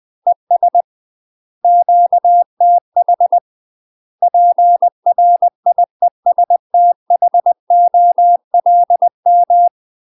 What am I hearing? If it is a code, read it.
Morse code, heard as a keyed tone, about 20 wpm, 700 Hz. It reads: ES QTH PRIESTHOLM
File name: 20WPM